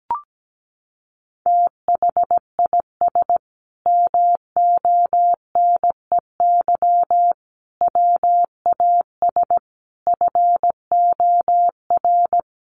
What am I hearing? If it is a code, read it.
Morse code, heard as a keyed tone, about 17 wpm, 700 Hz. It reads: THIS MONEY WAS FOR
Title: Sentences from Top 400 Words 17wpm